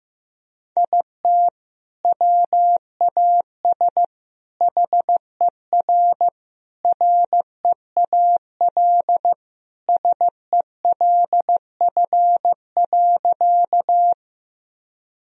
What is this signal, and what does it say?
Morse code, heard as a keyed tone, about 15 wpm, 700 Hz. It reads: IT WAS HER REAL SELF.
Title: Sentences from Top 1000 Words - Review Part 1 - 15wpm